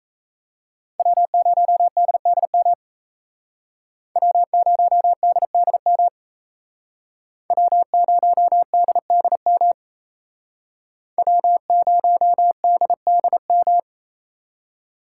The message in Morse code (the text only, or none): W0DDM W0DDM W0DDM W0DDM